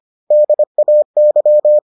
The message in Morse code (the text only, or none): DAY